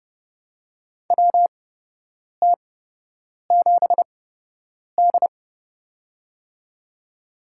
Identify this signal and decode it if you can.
Morse code, heard as a keyed tone, about 30 wpm, spaced as for 8 wpm, 700 Hz. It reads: WT7D
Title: USA Call Signs - 4 Characters - 8x Character Spacing - 30wpm